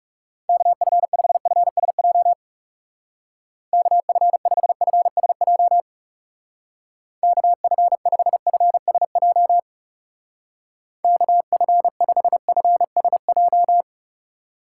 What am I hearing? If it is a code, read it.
Morse code, heard as a keyed tone, about 34 wpm, 700 Hz. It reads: KF5FSJ KF5FSJ KF5FSJ KF5FSJ
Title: Speed Racing - USA 2x3 Call Signs 30wpm